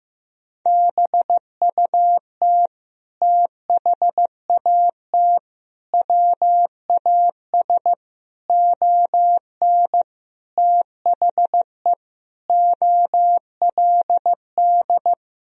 Morse code, heard as a keyed tone, about 15 wpm, 700 Hz. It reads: BUT THAT WAS ON THE OLD
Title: Sentences from Top 1000 Words - Review Part 1 - 15wpm